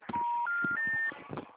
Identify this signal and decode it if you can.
Special information tone, with noise.
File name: out- -2010- - - .21149